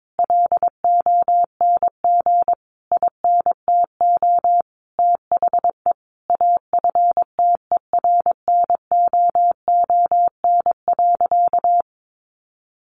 Morse code, heard as a keyed tone, about 22 wpm, 700 Hz. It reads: LONG INTO THE AFTERNOON.
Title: Sentences from Top 1500 Words 22wpm